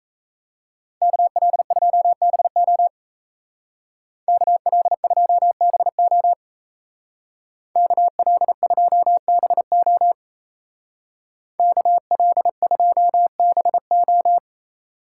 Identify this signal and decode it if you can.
Morse code, heard as a keyed tone, about 34 wpm, 700 Hz. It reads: KL2BO KL2BO KL2BO KL2BO
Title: Speed Racing - USA 2x2 Call Signs 28wpm